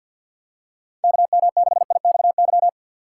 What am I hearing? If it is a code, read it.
Morse code, heard as a keyed tone, about 47 wpm, 700 Hz. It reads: KM6IXY
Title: USA Call Signs - 6 Characters - 4x Character Spacing - 50wpm